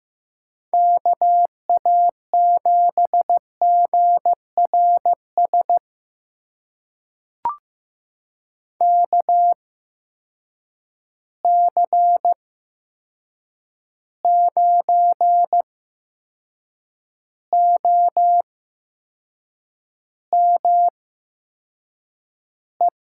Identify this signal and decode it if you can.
Morse code, heard as a keyed tone, about 15 wpm, spaced as for 6 wpm, 700 Hz. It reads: KA7GRS KC9OME
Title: USA Call Signs - 6 Characters - 8x Character Spacing - 15wpm